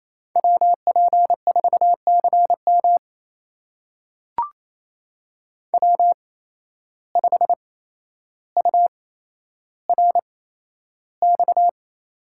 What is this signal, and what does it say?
Morse code, heard as a keyed tone, about 28 wpm, spaced as for 12 wpm, 700 Hz. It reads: WP4CM W5URX
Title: USA Call Signs - 8x Character Spacing - 28wpm